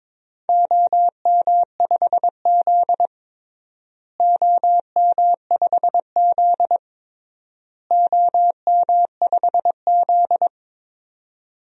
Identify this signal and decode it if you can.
Morse code, heard as a keyed tone, about 22 wpm, 700 Hz. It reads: OM5Z OM5Z OM5Z
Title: DXCC-100 Call Signs - 4 Characters - Repeated 3x - 22wpm